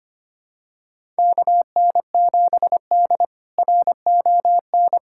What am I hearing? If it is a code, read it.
Morse code, heard as a keyed tone, about 25 wpm, 700 Hz. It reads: KN7D RON
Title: K1USN SST QSO Practice - 25wpm